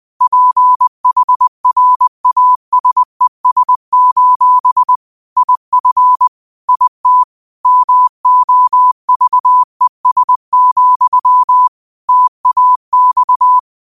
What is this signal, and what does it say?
Morse code, heard as a keyed tone, about 20 wpm, 1000 Hz. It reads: PHRASES: IF IT MOVES, TAX
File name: qotdfiles / 20wpm